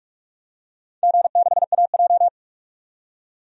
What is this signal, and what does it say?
Morse code, heard as a keyed tone, about 44 wpm, 700 Hz. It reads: G6AJ
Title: DXCC-100 Call Signs - 4 Characters - 45wpm